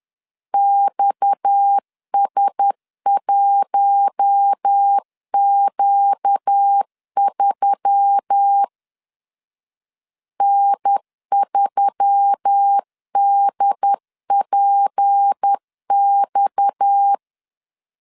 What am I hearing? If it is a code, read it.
Morse code, heard as a keyed tone, about 10 wpm, 800 Hz. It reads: XS1Q3 N3DPX
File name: LEZIONI / 6 PKX0 / 6 PKX 10wpm 2426